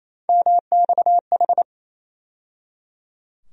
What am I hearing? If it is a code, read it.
Morse code, heard as a keyed tone, about 28 wpm, 700 Hz. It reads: MXH
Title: USA Call Sign Postfixes - 3 Letters 28wpm